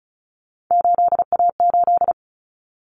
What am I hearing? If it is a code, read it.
Morse code, heard as a keyed tone, about 34 wpm, 700 Hz. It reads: 8A8
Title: Common Cut Numbers - 3-Digit 35wpm